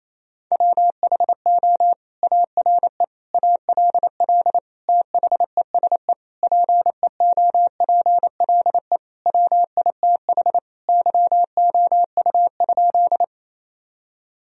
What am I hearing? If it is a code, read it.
Morse code, heard as a keyed tone, about 28 wpm, 700 Hz. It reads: WHO ARE ALL THESE PEOPLE WITH YOU?
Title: Sentences from Top 1000 Words - Review Part 1 - 28wpm